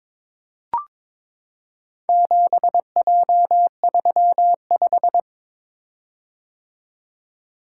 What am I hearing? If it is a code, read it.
Morse code, heard as a keyed tone, about 23 wpm, 700 Hz. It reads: 7J35
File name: practice / 22wpm / Four Letter-Number Combinations - 22wpm